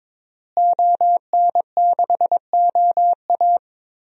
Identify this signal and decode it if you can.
Morse code, heard as a keyed tone, about 22 wpm, 700 Hz. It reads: ON6OA
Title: DXCC-100 Call Signs - 5 Characters - Repeated 3x - 22wpm